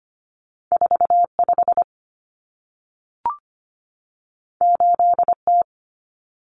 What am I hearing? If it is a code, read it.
Morse code, heard as a keyed tone, about 25 wpm, 700 Hz. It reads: 45 8T
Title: Common Cut Numbers - 2-Digit 25wpm